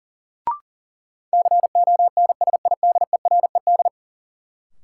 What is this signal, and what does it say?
Morse code, heard as a keyed tone, about 40 wpm, 700 Hz. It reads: CONSIDERED
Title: Sets of 5 Words - New Word Drill - 40wpm